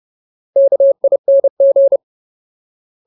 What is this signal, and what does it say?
Morse code, heard as a keyed tone, about 30 wpm, 550 Hz. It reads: KING
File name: Ultimate Word List / 30WPM Ultimate